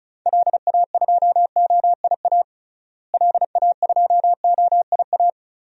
Morse code, heard as a keyed tone, about 36 wpm, 700 Hz. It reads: LA2OIA LA2OIA
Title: DXCC-100 Call Signs - 6 Characters - Repeated 2x - 35wpm